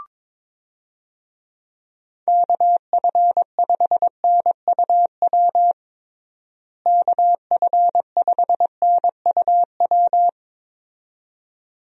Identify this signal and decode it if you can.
Morse code, heard as a keyed tone, about 21 wpm, 700 Hz. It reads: KF5NUW KF5NUW
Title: USA Call Signs - 6 Characters - Repeated 2x - 22wpm